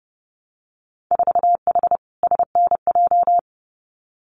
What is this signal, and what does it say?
Morse code, heard as a keyed tone, about 31 wpm, 700 Hz. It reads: 4H SNJ
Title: Winter Field Day Exchanges Encore 30wpm